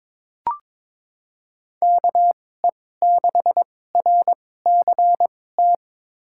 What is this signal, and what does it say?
Morse code, heard as a keyed tone, about 22 wpm, spaced as for 16 wpm, 700 Hz. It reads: KE6RCT
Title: USA Call Signs - 6 Characters - 2x Character Spacing - 22wpm